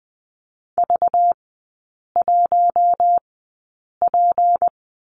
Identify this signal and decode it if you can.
Morse code, heard as a keyed tone, about 20 wpm, spaced as for 8 wpm, 700 Hz. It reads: V1P
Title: 191 Lesson 31e - Letter v - Three Characters 20wpm